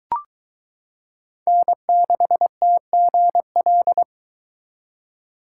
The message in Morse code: N6TGL

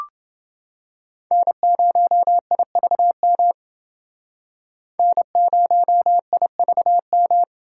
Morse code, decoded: N0IVM N0IVM

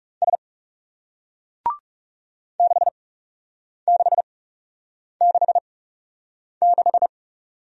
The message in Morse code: S 6666

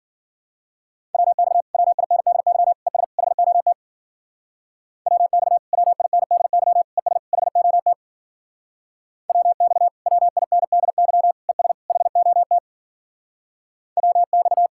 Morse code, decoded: WX WINDY ES HOT WX WINDY ES HOT WX WINDY ES HOT WX

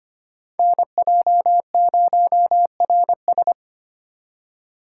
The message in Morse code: NJ0RS

